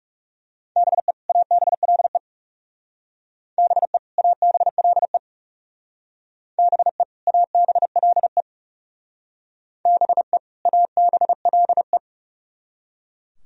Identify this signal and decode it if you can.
Morse code, heard as a keyed tone, about 37 wpm, 700 Hz. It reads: BE ABLE BE ABLE BE ABLE BE ABLE
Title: Speed-Racing - Sets of 2 Words 30wpm